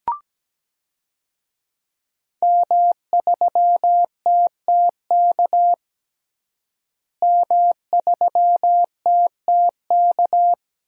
Morse code, heard as a keyed tone, about 17 wpm, 700 Hz. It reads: M3TTK M3TTK
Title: DXCC-100 Call Signs - 5 Characters - Repeated 2x - 17wpm